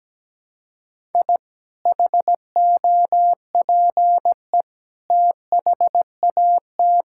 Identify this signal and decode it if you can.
Morse code, heard as a keyed tone, about 17 wpm, 700 Hz. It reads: I HOPE THAT
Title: Sentences from Top 1000 Words - Review Part 5 - 17wpm